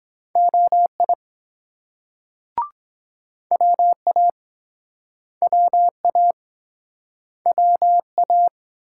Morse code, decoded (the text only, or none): OI WA WA WA